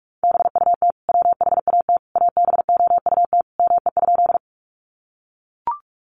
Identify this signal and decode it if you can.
Morse code, heard as a keyed tone, about 44 wpm, 700 Hz. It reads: BUT WHAT ABOUT ME?
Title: Sentences from Top 200 Words 45wpm